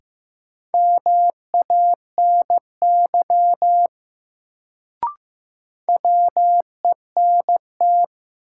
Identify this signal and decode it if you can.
Morse code, heard as a keyed tone, about 15 wpm, 700 Hz. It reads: MANY WENT
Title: Sets of 2 Words - New Word Drill - 15wpm